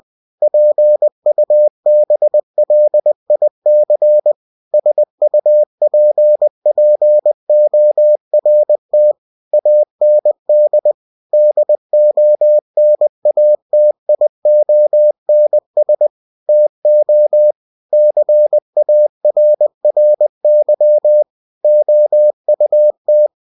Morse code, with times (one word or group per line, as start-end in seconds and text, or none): 0.4-4.3 PUBLIC
4.7-9.1 SUPPORT
9.5-10.9 AND
11.3-16.1 DONATIONS
16.5-17.5 TO
17.9-21.2 CARRY
21.6-23.3 OUT